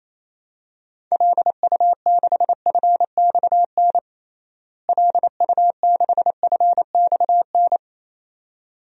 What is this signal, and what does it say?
Morse code, heard as a keyed tone, about 28 wpm, 700 Hz. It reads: LU6FXN LU6FXN
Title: DXCC-100 Call Signs - 6 Characters - Repeated 2x - 28wpm